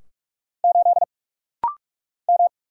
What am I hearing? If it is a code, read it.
Morse code, heard as a keyed tone, about 44 wpm, 700 Hz. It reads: 8 M
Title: Single Letter-Number Alt - Rapid-Fire 45wpm